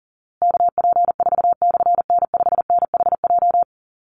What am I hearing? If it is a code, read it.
Morse code, heard as a keyed tone, about 40 wpm, 700 Hz. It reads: KP4/N5NHJ
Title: Call Signs Repeated 3x - 40wpm